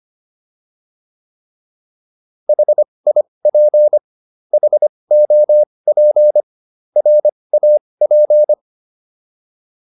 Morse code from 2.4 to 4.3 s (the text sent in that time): HIP